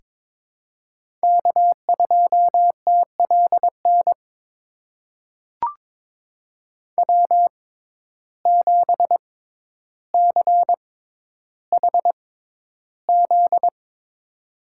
Morse code, heard as a keyed tone, about 22 wpm, spaced as for 11 wpm, 700 Hz. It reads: K2TLN W7CHZ
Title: USA Call Signs - 5 Characters - 6x Character Spacing - 22wpm